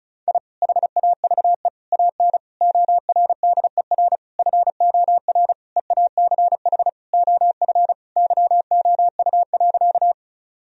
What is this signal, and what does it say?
Morse code, heard as a keyed tone, about 36 wpm, 700 Hz. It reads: I HAVE AN ORDER FOR EACH OF YOU.
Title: Sentences from Top 1000 Words - Review Part 2 - 35wpm